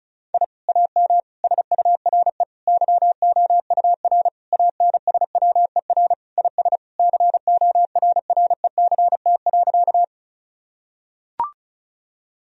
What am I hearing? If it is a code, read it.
Morse code, heard as a keyed tone, about 36 wpm, 700 Hz. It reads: I AM SURE YOUR ANSWER IS CORRECT.
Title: Sentences from Top 1000 Words - Review Part 2 - 35wpm